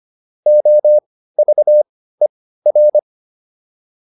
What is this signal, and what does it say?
Morse code, heard as a keyed tone, about 25 wpm, spaced as for 15 wpm, 600 Hz. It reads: OVER
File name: PR305 15